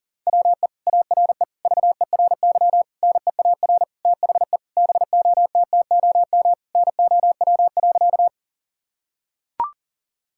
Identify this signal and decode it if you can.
Morse code, heard as a keyed tone, about 40 wpm, 700 Hz. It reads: WE ARE VERY NEAR THE BOTTOM NOW.
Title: Sentences from Top 1000 Words - Review Part 6 - 40wpm